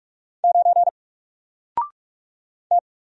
Morse code, heard as a keyed tone, about 47 wpm, 700 Hz. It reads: 9 T
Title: Single Letter-Number 45wpm